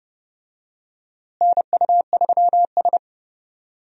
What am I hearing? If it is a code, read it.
Morse code, heard as a keyed tone, about 30 wpm, 700 Hz. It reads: NU3S
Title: USA Call Signs - 4 Characters - Repeated 3x - 30wpm